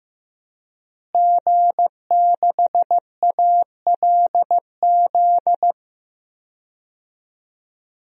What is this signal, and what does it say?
Morse code, heard as a keyed tone, about 15 wpm, 700 Hz. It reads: G6ALZ
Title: DXCC-100 Call Signs - 5 Characters - Repeated 2x - 15wpm